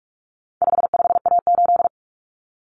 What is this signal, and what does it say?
Morse code, heard as a keyed tone, about 44 wpm, 700 Hz. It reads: 55A8
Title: Common Cut Numbers - 4-Digit 45wpm